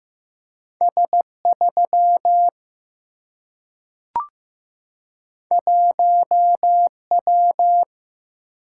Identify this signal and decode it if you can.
Morse code, heard as a keyed tone, about 15 wpm, 700 Hz. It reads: S 3 1 W
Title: Single-Letter Single-Number 15wpm